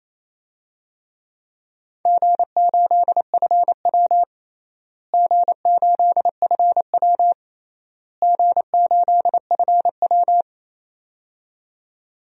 Morse code, heard as a keyed tone, about 27 wpm, 700 Hz. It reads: G8FW G8FW G8FW
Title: DXCC-100 Call Signs - 4 Characters - Repeated 3x - 28wpm